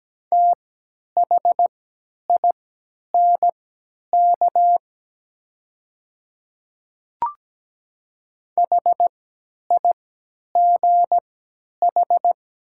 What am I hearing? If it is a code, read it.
Morse code, heard as a keyed tone, about 17 wpm, spaced as for 10 wpm, 700 Hz. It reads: THINK HIGH
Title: 3x Character Spacing - Top 300 Words - 17wpm